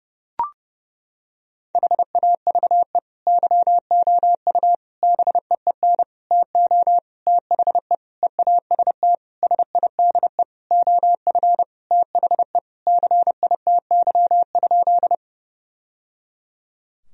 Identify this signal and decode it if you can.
Morse code, heard as a keyed tone, about 30 wpm, 700 Hz. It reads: HAVE YOU BEEN TO THE EAST SIDE OF THE CITY?
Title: Sentences from Top 1000 Words - Review Part 3 - 30wpm